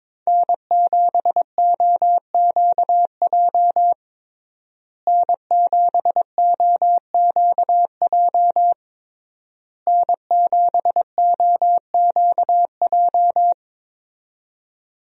Morse code, heard as a keyed tone, about 22 wpm, 690 Hz. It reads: N7OQJ N7OQJ N7OQJ
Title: USA Call Signs - 5 Characters - Repeated 3x - 22wpm